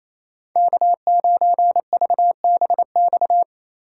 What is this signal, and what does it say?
Morse code, heard as a keyed tone, about 34 wpm, 700 Hz. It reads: K9VBX
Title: Speed Racing - USA 1x3 Call Signs 28wpm